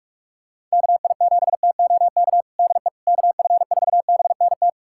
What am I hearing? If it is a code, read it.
Morse code, heard as a keyed tone, about 43 wpm, 700 Hz. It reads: KI7TOK DE KF4BNT
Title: USA Callsign DE Callsign - 6 Characters - 45wpm